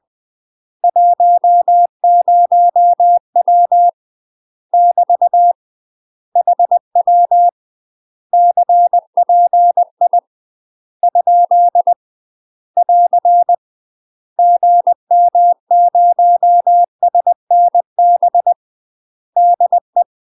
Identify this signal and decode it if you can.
Morse code, heard as a keyed tone, about 20 wpm, 700 Hz. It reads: 10W <BT> HW CPI ? <AR> GM0SNB DE
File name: UK MASTER QSOs TRAINING / 20WPM / UK MASTER QSOs 0023